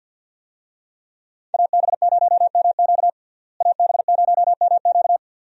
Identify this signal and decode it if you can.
Morse code, heard as a keyed tone, about 49 wpm, 700 Hz. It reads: AB0MQ AB0MQ
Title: USA Call Signs - 5 Characters - Repeated 2x - 50wpm